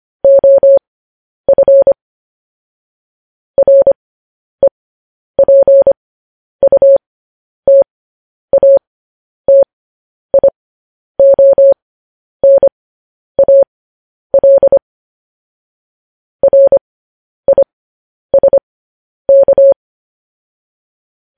Morse code, heard as a keyed tone, about 25 wpm, spaced as for 10 wpm, 550 Hz. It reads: OF REPUTATIONAL RISK